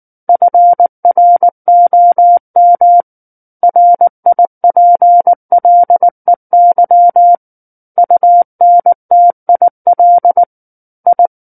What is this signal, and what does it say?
Morse code, heard as a keyed tone, about 19 wpm, 700 Hz. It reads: FROM RIPLEY UNTIL I
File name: Book One / 10-Chapter 10 - 19 WPM